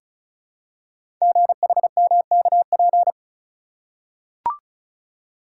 Morse code, heard as a keyed tone, about 36 wpm, 700 Hz. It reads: GHMKP
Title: Code Groups - 5 Letters - 35wpm